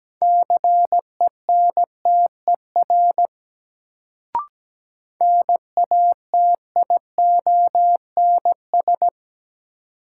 Morse code, heard as a keyed tone, about 17 wpm, 700 Hz. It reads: CENTER NATIONS
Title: Sets of 3 Words - New Word Drill - 17wpm